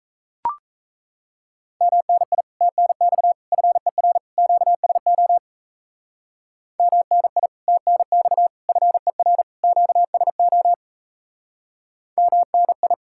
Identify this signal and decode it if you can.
Morse code, heard as a keyed tone, about 35 wpm, 700 Hz. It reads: MNI TNX FER QSO MNI TNX FER QSO MNI
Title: Speed-Racing Common QSO Phrases - 28wpm